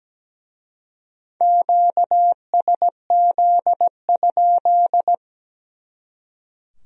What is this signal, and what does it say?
Morse code, heard as a keyed tone, about 20 wpm, 700 Hz. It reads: QSZ?
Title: Speed-Racing Q-Codes - 17wpm